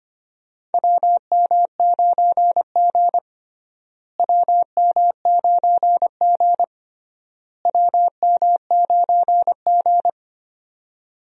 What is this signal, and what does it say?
Morse code, heard as a keyed tone, about 25 wpm, 700 Hz. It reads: WM9G WM9G WM9G
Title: USA Call Signs - 4 Characters - Repeated 3x - 25wpm